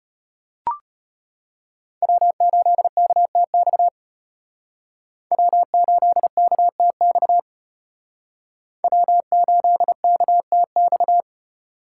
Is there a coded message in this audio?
Yes: Morse code from 2.0 to 11.2 s, heard as a keyed tone, about 32 wpm, 700 Hz. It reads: W8KTX W8KTX W8KTX